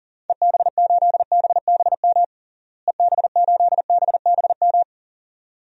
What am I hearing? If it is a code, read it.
Morse code, heard as a keyed tone, about 40 wpm, 700 Hz. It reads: EB8BBM EB8BBM
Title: DXCC-100 Call Signs - 6 Characters - Repeated 2x - 40wpm